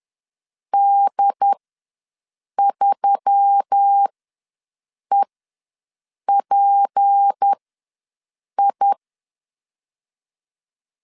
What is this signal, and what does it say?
Morse code, heard as a keyed tone, about 10 wpm, spaced as for 6 wpm, 800 Hz. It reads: D3EPI
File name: LEZIONI / 7 GF9Z / 7 GF9Z ws 10wpm 3919